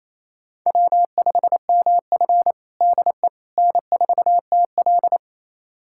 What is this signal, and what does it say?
Morse code, heard as a keyed tone, about 28 wpm, 690 Hz. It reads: W5MF DE N4TL
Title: USA Callsign DE Callsign - 4 Characters - 28wpm